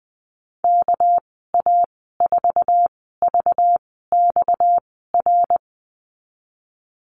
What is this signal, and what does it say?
Morse code, heard as a keyed tone, about 20 wpm, spaced as for 14 wpm, 700 Hz. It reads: KA4VXR
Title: 245 Lesson 39g - Letter z - Call Signs 20wpm